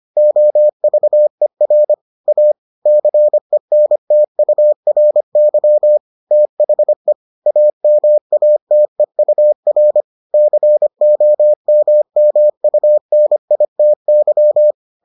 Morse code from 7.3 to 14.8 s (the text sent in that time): AMATEUR COMMUNITY